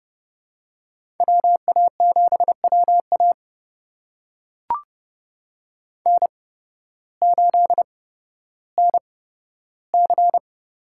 Morse code, heard as a keyed tone, about 32 wpm, spaced as for 14 wpm, 700 Hz. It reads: WA7WA N8NC
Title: USA Call Signs - 5 Characters - 8x Character Spacing - 30wpm